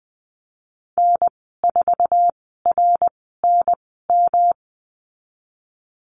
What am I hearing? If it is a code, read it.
Morse code, heard as a keyed tone, about 20 wpm, spaced as for 15 wpm, 700 Hz. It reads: N4RNM
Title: 147 Lesson 24g - Letter y - Call Signs 20wpm